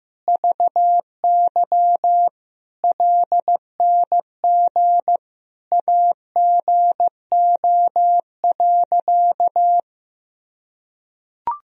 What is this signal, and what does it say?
Morse code, heard as a keyed tone, about 16 wpm, 700 Hz. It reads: VY LNG AGO.
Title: Sentences from Top 500 Words and Abbreviations 15wpm